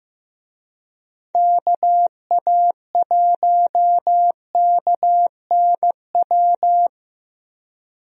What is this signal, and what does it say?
Morse code, heard as a keyed tone, about 15 wpm, 700 Hz. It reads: KA1KNW
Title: USA Call Signs - 6 Characters - Repeated 2x - 15wpm